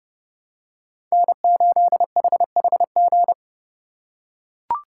Morse code, heard as a keyed tone, about 32 wpm, 700 Hz. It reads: N8HHG
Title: K1USN SST QSO Practice - 30wpm